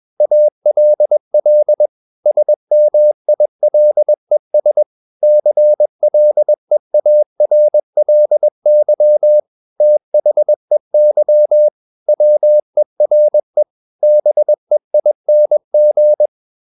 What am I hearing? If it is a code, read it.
Morse code, heard as a keyed tone, about 21 wpm, 600 Hz. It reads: ALL SMILES CLEARLY THEY WERE BEING
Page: CW med Gnister